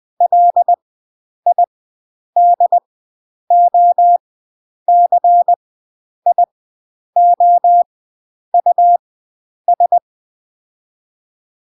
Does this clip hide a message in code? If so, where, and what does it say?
Yes: Morse code from 0.2 to 10.0 s, heard as a keyed tone, about 20 wpm, spaced as for 9 wpm, 700 Hz. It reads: LIDOCIOUS